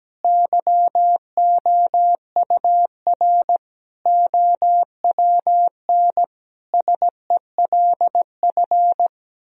Morse code, heard as a keyed tone, about 17 wpm, 700 Hz. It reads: YOUR OWN SELF
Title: Sentences from Top 1000 Words - Review Part 1 - 17wpm